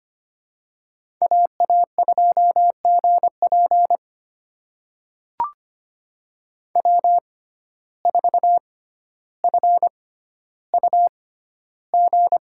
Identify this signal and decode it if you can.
Morse code, heard as a keyed tone, about 25 wpm, spaced as for 13 wpm, 700 Hz. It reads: AA2GP W4FUG
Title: USA Call Signs - 5 Characters - 6x Character Spacing - 25wpm